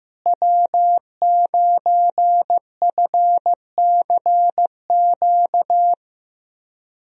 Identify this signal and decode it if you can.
Morse code, heard as a keyed tone, about 15 wpm, 700 Hz. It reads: W9FCQ
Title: USA Call Signs - 5 Characters - Repeated 3x - 15wpm